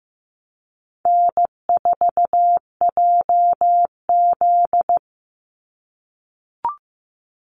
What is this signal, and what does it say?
Morse code, heard as a keyed tone, about 15 wpm, 700 Hz. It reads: N4JZ
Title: USA Call Signs 15wpm